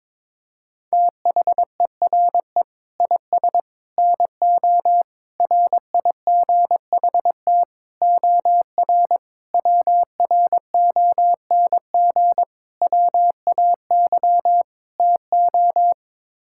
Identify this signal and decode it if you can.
Morse code, heard as a keyed tone, about 22 wpm, 700 Hz. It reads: THERE IS NO RIGHT OR WRONG WAY TO
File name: practice / 22wpm / Sets of 8 Words - 22wpm